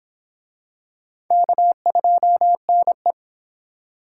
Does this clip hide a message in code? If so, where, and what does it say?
Yes: Morse code from 1.3 to 3.1 s, heard as a keyed tone, about 25 wpm, 700 Hz. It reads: K2NE